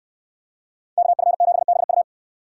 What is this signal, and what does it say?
Morse code, heard as a keyed tone, about 66 wpm, 700 Hz. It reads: KF7BU